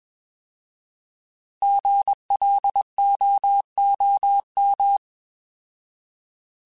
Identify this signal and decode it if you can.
Morse code, heard as a keyed tone, about 20 wpm, 800 Hz. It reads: GLOOM
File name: Alphabetical / Alphabetical mp3s / Gs @20